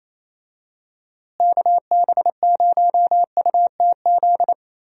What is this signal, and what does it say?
Morse code, heard as a keyed tone, about 28 wpm, 700 Hz. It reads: KB0UTZ
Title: USA Call Signs - 6 Characters - Repeated 3x - 28wpm